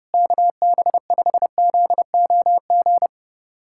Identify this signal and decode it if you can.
Morse code, heard as a keyed tone, about 30 wpm, 700 Hz. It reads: KB5ZOG
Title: USA Callsign DE Callsign - 6 Characters - 30wpm